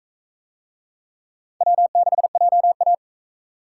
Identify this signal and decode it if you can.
Morse code, heard as a keyed tone, about 39 wpm, 700 Hz. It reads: W6JA